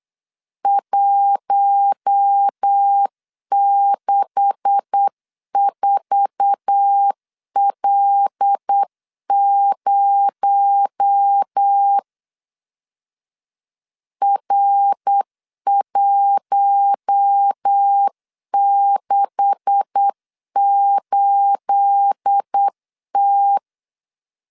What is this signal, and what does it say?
Morse code, heard as a keyed tone, about 8 wpm, 800 Hz. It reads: 164L0 R168T
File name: LEZIONI / 9 286 8wpm 3425